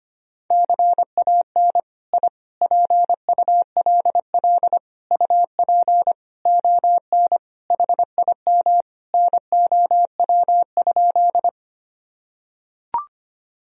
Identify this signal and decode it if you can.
Morse code, heard as a keyed tone, about 25 wpm, 700 Hz. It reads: CAN I PULL UP ON HIM NOW?
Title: Sentences from Top 1000 Words - Review Part 2 - 25wpm